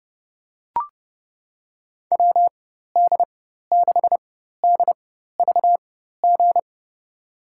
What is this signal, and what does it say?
Morse code, heard as a keyed tone, about 29 wpm, spaced as for 14 wpm, 700 Hz. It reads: WD6DVG
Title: USA Call Signs - 6 Characters - 4x Character Spacing - 30wpm